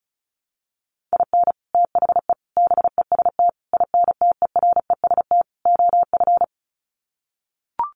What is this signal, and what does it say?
Morse code, heard as a keyed tone, about 34 wpm, 700 Hz. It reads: IN THE BEST INTEREST OF
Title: Sets of 5 Words Encore 35wpm